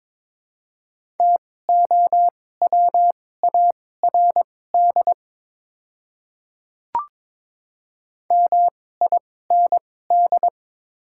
Morse code, heard as a keyed tone, about 22 wpm, spaced as for 16 wpm, 700 Hz. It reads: TOWARD MIND
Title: 2x Character Spacing - Top 500 Words - 22wpm